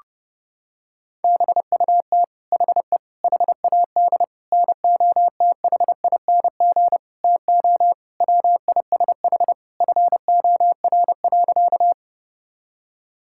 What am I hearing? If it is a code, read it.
Morse code, heard as a keyed tone, about 30 wpm, 700 Hz. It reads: BUT HE HAD NOTHING TO WISH FOR.
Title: Sentences from Top 1600 Words 30wpm